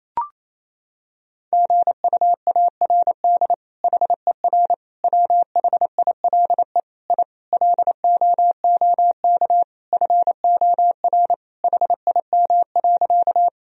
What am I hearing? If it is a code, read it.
Morse code, heard as a keyed tone, about 27 wpm, 700 Hz. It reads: GUARD HER WHILE I LOOK FOR HIM.
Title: Sentences from Top 1000 Words - Review Part 4 - 28wpm